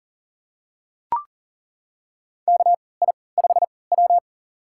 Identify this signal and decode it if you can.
Morse code, heard as a keyed tone, about 40 wpm, spaced as for 23 wpm, 700 Hz. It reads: KI5W
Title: USA Call Signs - 3x Character Spacing - 40wpm